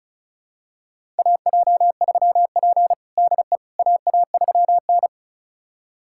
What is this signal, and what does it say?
Morse code, heard as a keyed tone, about 36 wpm, 700 Hz. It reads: AJ3P DE AA3N
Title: USA Callsign DE Callsign - 4 Characters - 35wpm